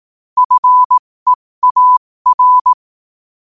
Morse code, heard as a keyed tone, about 18 wpm, spaced as for 15 wpm, 1000 Hz. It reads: FEAR
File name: qotdfiles / 15wpm